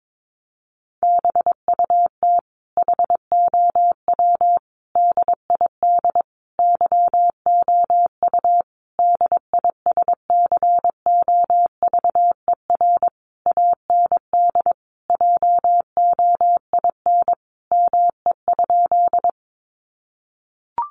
Morse code, heard as a keyed tone, about 22 wpm, 700 Hz. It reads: BUT HOW DID YOU DISCOVER AND JOIN ME?
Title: Sentences from Top 1400 Words 22wpm